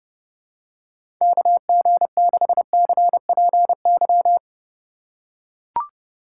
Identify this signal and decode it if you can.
Morse code, heard as a keyed tone, about 32 wpm, 700 Hz. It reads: KG6CPY
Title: USA Call Signs - 6 Characters - 3x Character Spacing - 30wpm